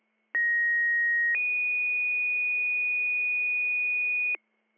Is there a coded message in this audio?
Yes: two-tone page from 0.3 to 4.4 s, heard as a one-second tone, then a three-second tone, about 1900 then 2450 Hz, with noise.